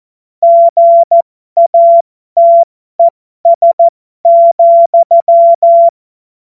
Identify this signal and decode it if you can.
Morse code, heard as a keyed tone, about 14 wpm, spaced as for 12 wpm, 680 Hz. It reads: GATES,